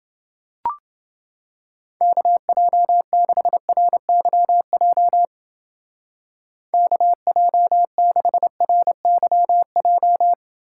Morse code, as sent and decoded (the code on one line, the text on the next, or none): -.- .--- -.... .-. -.-- .--- / -.- .--- -.... .-. -.-- .---
KJ6RYJ KJ6RYJ